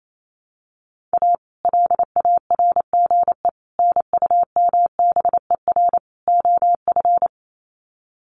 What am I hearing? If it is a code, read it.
Morse code, heard as a keyed tone, about 28 wpm, 700 Hz. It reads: A LARGE NUMBER OF
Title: Sets of 4 Words 28wpm